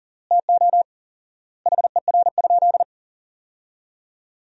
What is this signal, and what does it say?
Morse code, heard as a keyed tone, about 37 wpm, 700 Hz. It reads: TO HER?
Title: 4x Word-Spacing - Sentences from Top 200 Words - 40wpm